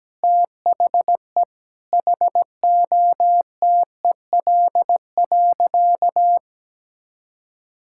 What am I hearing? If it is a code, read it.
Morse code, heard as a keyed tone, about 17 wpm, 700 Hz. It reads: THE HOTEL.
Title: Sentences from Top 1600 Words 17wpm